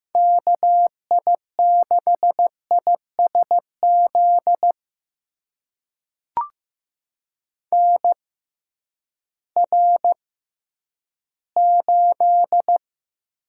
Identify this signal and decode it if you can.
Morse code, heard as a keyed tone, about 15 wpm, spaced as for 10 wpm, 700 Hz. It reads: KI6ISZ NR8
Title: USA Call Signs - 6x Character Spacing - 15wpm